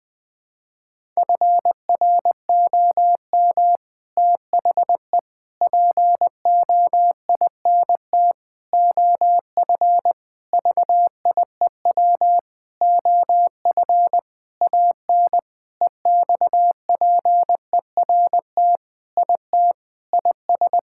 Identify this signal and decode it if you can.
Morse code, heard as a keyed tone, about 21 wpm, 700 Hz. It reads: FROM THE POINT OF VIEW OF AN EXPERT IT IS
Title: Sentences with 6-Word Phrases - 20wpm